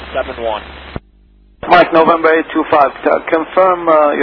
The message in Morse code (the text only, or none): none